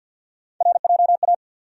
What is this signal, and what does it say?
Morse code, heard as a keyed tone, about 53 wpm, 700 Hz. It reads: RJA
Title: USA Call Sign Postfixes - 3 Letters 50wpm